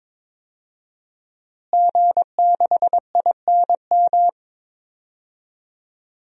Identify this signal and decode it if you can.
Morse code, heard as a keyed tone, about 22 wpm, 700 Hz. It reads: G6INM
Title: DXCC-100 Call Signs - 5 Characters - Repeated 2x - 22wpm